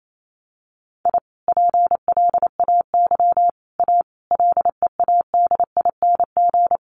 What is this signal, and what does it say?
Morse code, heard as a keyed tone, about 28 wpm, 700 Hz. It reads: I PLAY A LEADING
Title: Sentences from Top 1100 Words 28wpm